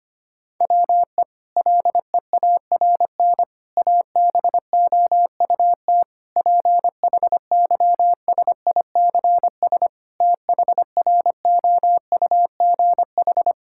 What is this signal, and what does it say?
Morse code, heard as a keyed tone, about 25 wpm, 700 Hz. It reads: WE LEARN ABOUT PHYSICS THROUGH
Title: Sentences from Top 1000 Words - Review Part 5 - 25wpm